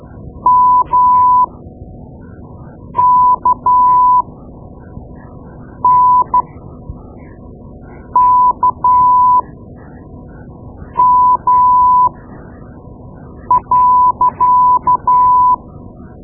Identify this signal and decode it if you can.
Morse code, heard as a keyed tone, about 13 wpm, spaced as for 5 wpm, 1000 Hz. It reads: MKNKM.
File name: Morse Lessons / Morse Lessons / Tape-2B